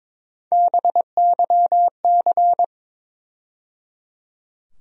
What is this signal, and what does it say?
Morse code, heard as a keyed tone, about 22 wpm, 700 Hz. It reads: BYC
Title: USA Call Sign Postfixes - 3 Letters 22wpm